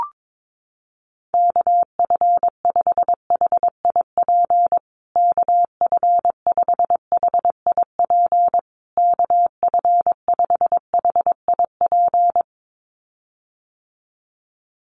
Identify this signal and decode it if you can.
Morse code, heard as a keyed tone, about 22 wpm, 700 Hz. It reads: KF5HIP KF5HIP KF5HIP
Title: USA Call Signs Repeated 3x 22wpm